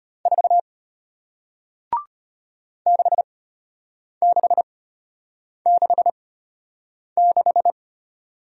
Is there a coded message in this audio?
Yes: Morse code from 0.3 to 7.7 s, heard as a keyed tone, about 33 wpm, spaced as for 8 wpm, 700 Hz. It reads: 4 6666